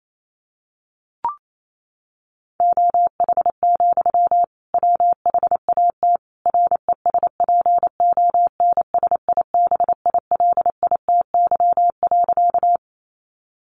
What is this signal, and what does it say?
Morse code, heard as a keyed tone, about 28 wpm, 700 Hz. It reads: OH, WHAT RESPONSIBILITY.
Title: Sentences from Top 1200 Words 28wpm